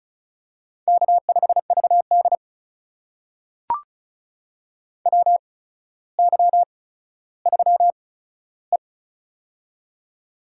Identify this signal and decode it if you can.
Morse code, heard as a keyed tone, about 36 wpm, spaced as for 15 wpm, 700 Hz. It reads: K5VD WY3E
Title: USA Call Signs - 4 Characters - 8x Character Spacing - 35wpm